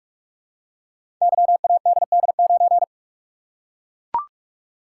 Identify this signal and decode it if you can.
Morse code, heard as a keyed tone, about 44 wpm, 700 Hz. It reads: YADD9
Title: Code Groups - 5 Letter:Numbers - 45wpm